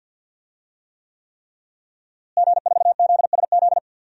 Morse code, heard as a keyed tone, about 47 wpm, 700 Hz. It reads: G4ZSZ
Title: DXCC-100 Call Signs - 5 Characters - Repeated 3x - 50wpm